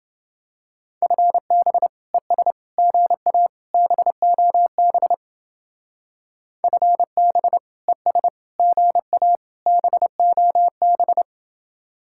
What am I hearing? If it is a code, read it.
Morse code, heard as a keyed tone, about 29 wpm, 700 Hz. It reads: FB ES GA BOB FB ES GA BOB